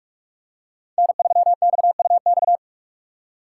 Morse code, heard as a keyed tone, about 44 wpm, 700 Hz. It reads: N3XUX